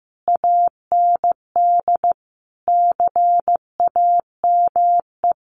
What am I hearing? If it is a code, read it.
Morse code, heard as a keyed tone, about 15 wpm, 700 Hz. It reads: AND CAME
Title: Sentences from Top 1100 Words 15wpm